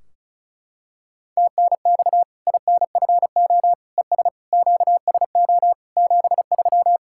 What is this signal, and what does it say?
Morse code, heard as a keyed tone, about 36 wpm, 700 Hz. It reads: TNX INFO ES QSO 73
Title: K1USN SST QSO Practice 35wpm